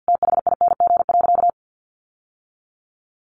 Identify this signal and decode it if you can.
Morse code, heard as a keyed tone, about 50 wpm, 700 Hz. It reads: THING.
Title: Sentences from Top 800 Words 50wpm